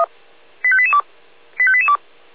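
Five-tone selective call (ZVEI), heard as quick runs of five tones, tones of about 70 ms. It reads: 85882 85882
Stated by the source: ZVEI five-tone call received by SDR.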